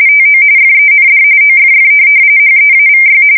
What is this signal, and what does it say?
RTTY, about 45.45 baud. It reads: TO FEND OFF HEAVY